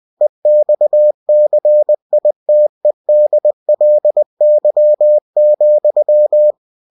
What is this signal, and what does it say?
Morse code, heard as a keyed tone, about 20 wpm, 600 Hz. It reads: EXCITEDLY,
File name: cw-books / winnie-the-pooh / winnie-20wpm-page-0087